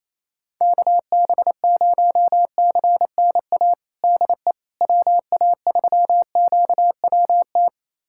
Morse code, heard as a keyed tone, about 28 wpm, 700 Hz. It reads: KB0CNA DE WA3QWT
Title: USA Callsign DE Callsign - 6 Characters - 28wpm